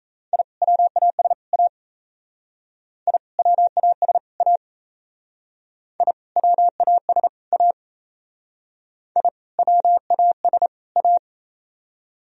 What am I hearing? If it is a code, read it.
Morse code, heard as a keyed tone, about 35 wpm, 700 Hz. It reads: I WAS A I WAS A I WAS A I WAS A
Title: Speed-Racing - Sets of 3 Words 28wpm